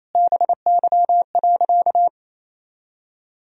Morse code, heard as a keyed tone, about 28 wpm, 700 Hz. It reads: BY.
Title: Sentences from Top 1000 Words - Review Part 1 - 28wpm